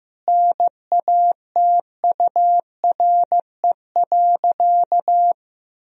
Morse code, heard as a keyed tone, about 15 wpm, 700 Hz. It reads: NATURE.
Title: Sentences from Top 1000 Words - Review Part 3 - 15wpm